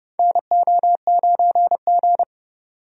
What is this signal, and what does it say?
Morse code, heard as a keyed tone, about 30 wpm, 700 Hz. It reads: NO9G
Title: USA Call Signs - 4 Characters - 30wpm